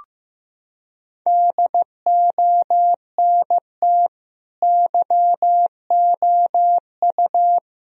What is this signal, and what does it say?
Morse code, heard as a keyed tone, about 15 wpm, 700 Hz. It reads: DONT YOU
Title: Sentences from Top 1000 Words - Review Part 2 - 15wpm